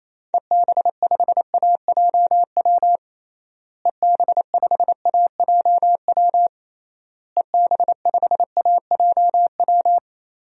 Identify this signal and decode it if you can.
Morse code, heard as a keyed tone, about 28 wpm, 700 Hz. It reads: EB5AJW EB5AJW EB5AJW
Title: DXCC-100 Call Signs - 6 Characters - Repeated 3x - 28wpm